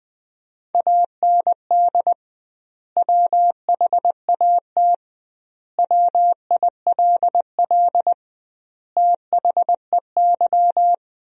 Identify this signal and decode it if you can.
Morse code, heard as a keyed tone, about 21 wpm, 700 Hz. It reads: AND WHAT WILL THEY
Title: 2x Word-Spacing - Sentences from Top 200 Words - 20wpm